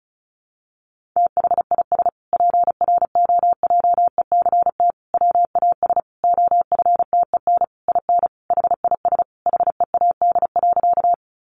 Morse code, heard as a keyed tone, about 34 wpm, 700 Hz. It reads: THIS PROJECT WAS OFTEN IN HIS HEAD.
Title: Sentences from Top 1400 Words 35wpm